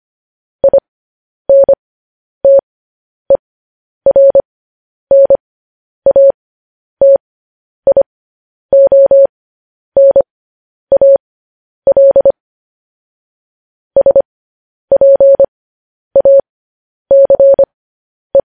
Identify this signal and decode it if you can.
Morse code, heard as a keyed tone, about 25 wpm, spaced as for 10 wpm, 550 Hz. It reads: INTERNATIONAL SPACE